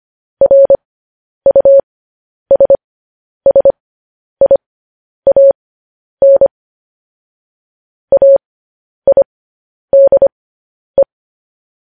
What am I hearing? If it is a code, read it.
Morse code, heard as a keyed tone, about 25 wpm, spaced as for 10 wpm, 550 Hz. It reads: RUSSIAN AIDE
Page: News Headlines in Morse Code at 10 WPM